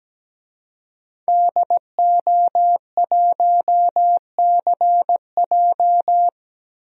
Morse code, heard as a keyed tone, about 17 wpm, 700 Hz. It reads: DO1CJ
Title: DXCC-100 Call Signs - 5 Characters - 17wpm